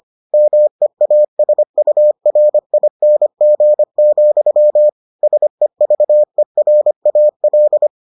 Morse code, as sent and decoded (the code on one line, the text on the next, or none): -- . .- ... ..- .-. .. -. --. --..-- / ... ...- . .-. .- .-..
MEASURING, SEVERAL